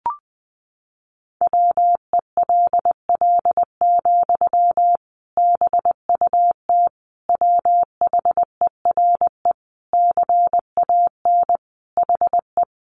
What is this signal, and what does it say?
Morse code, heard as a keyed tone, about 20 wpm, 700 Hz. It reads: WELL, BUT WHERE CAN HE
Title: Sentences from Top 500 Words 20wpm